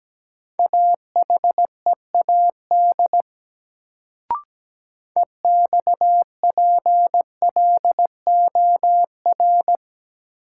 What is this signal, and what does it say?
Morse code, heard as a keyed tone, about 17 wpm, 700 Hz. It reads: AHEAD EXPLOR
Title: Sentences from Top 1100 Words - New Word Drill Alt - 17wpm